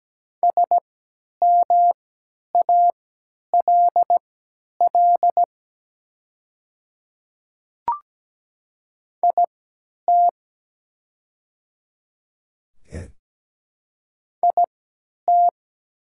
Morse code, heard as a keyed tone, about 17 wpm, spaced as for 10 wpm, 700 Hz. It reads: SMALL IT IT
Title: 3x Character Spacing - Top 200 Words - 17wpm